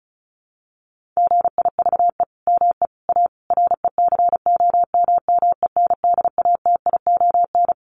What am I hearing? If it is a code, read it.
Morse code, heard as a keyed tone, about 34 wpm, 700 Hz. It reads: GIVE ME A RECOMMENDATION
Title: Sentences from Top 1400 Words 35wpm